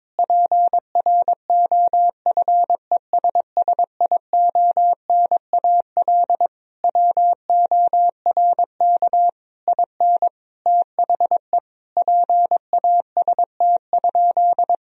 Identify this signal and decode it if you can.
Morse code, heard as a keyed tone, about 22 wpm, 700 Hz. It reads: PROFESSIONAL WORK IN THE PAST?
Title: Sentences from Top 1000 Words - Review Part 4 - 22wpm